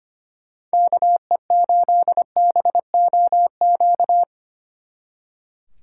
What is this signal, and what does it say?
Morse code, heard as a keyed tone, about 30 wpm, 700 Hz. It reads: KE8BOQ
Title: Speed Racing - USA 2x3 Call Signs 25wpm